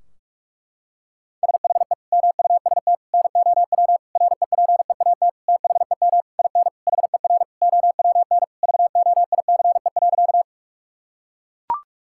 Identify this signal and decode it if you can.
Morse code, heard as a keyed tone, about 41 wpm, 700 Hz. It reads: SHE MUST NOW REPEAT THEM IN HER OWN VOICE.
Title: Sentences from Top 1600 Words 45wpm